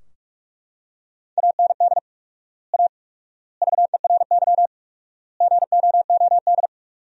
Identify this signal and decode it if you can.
Morse code, heard as a keyed tone, about 42 wpm, 700 Hz. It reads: AND A VERY GOOD
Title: 4x Word-Spacing - Sentences from Top 200 Words - 45wpm